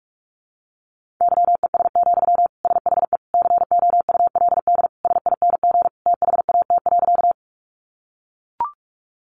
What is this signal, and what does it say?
Morse code, heard as a keyed tone, about 44 wpm, 700 Hz. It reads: YES, SHE COULD SING THAT.
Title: Sentences from Top 500 Words 45wpm